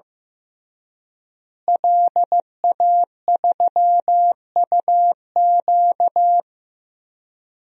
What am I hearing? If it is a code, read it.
Morse code, heard as a keyed tone, about 16 wpm, 700 Hz. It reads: LA3UQ
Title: DXCC-100 Call Signs - 5 Characters - Repeated 2x - 15wpm